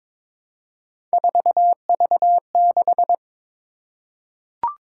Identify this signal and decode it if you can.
Morse code, heard as a keyed tone, about 22 wpm, 700 Hz. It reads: 4V6
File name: practice / 22wpm / Three Letter-Number Combinations - 22wpm